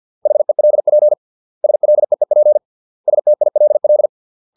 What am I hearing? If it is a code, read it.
Morse code, heard as a keyed tone, about 50 wpm, 600 Hz. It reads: HELP SLEEP STILL